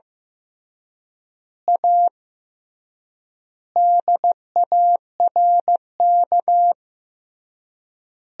Morse code, heard as a keyed tone, about 15 wpm, 700 Hz. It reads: A DARK
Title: 3x Word-Spacing - Sentences from Top 500 Words - 15wpm